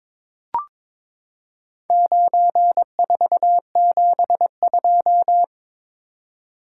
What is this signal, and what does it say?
Morse code, heard as a keyed tone, about 22 wpm, 700 Hz. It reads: 9472
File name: practice / 22wpm / Four Letter-Number Combinations - 22wpm